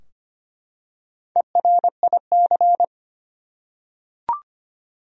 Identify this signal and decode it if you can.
Morse code, heard as a keyed tone, about 25 wpm, 700 Hz. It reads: ERIC
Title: USA Common Names - Max 5 Letters - 25wpm